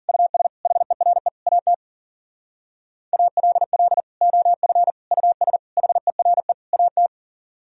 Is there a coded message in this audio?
Yes: Morse code from 0.1 to 7.1 s, heard as a keyed tone, about 53 wpm, 700 Hz. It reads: US HERE AT ALL OF US HERE AT